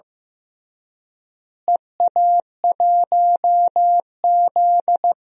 Morse code, heard as a keyed tone, about 16 wpm, 700 Hz. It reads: EA1Z
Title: DXCC-100 Call Signs - 4 Characters - Repeated 3x - 15wpm